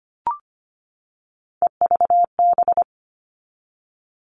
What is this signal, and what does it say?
Morse code, heard as a keyed tone, about 25 wpm, 700 Hz. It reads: EVB
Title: All Cut Numbers - 3-Digit 25wpm